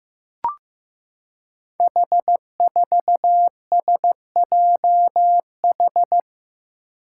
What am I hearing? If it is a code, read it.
Morse code, heard as a keyed tone, about 15 wpm, 700 Hz. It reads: H4SJH
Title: Five Challenging Letter-Number Combinations 15wpm